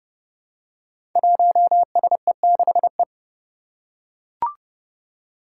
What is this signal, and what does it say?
Morse code, heard as a keyed tone, about 30 wpm, 700 Hz. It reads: 1SE6E
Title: Five Challenging Letter-Number Combinations 30wpm